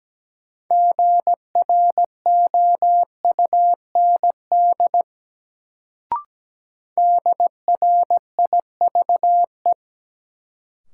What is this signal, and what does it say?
Morse code, heard as a keyed tone, about 17 wpm, 700 Hz. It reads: GROUND DRIVE
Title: Sentences from Top 500 Words - New Word Drill - 17wpm